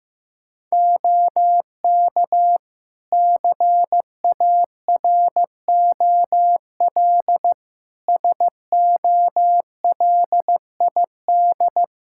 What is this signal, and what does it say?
Morse code, heard as a keyed tone, about 15 wpm, 700 Hz. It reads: OK CAROL SOLID
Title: K1USN SST QSO Practice 15wpm